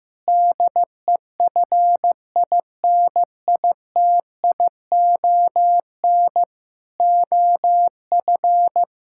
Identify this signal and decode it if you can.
Morse code, heard as a keyed tone, about 15 wpm, 700 Hz. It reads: DEFINITION OF
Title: Sentences from Top 1000 Words - Review Part 6 - 15wpm